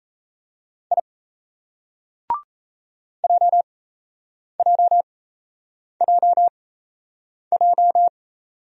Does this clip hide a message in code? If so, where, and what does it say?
Yes: Morse code from 0.9 to 8.1 s, heard as a keyed tone, about 35 wpm, spaced as for 8 wpm, 700 Hz. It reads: I JJJJ